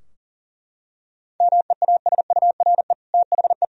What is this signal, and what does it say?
Morse code, heard as a keyed tone, about 40 wpm, 700 Hz. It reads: MEASURE THE
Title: Sentences from Top 1000 Words - Review Part 2 - 40wpm